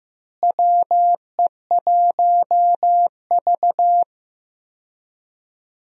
Morse code, heard as a keyed tone, about 15 wpm, 700 Hz. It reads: WE1V
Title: USA Call Signs - 4 Characters - Repeated 3x - 15wpm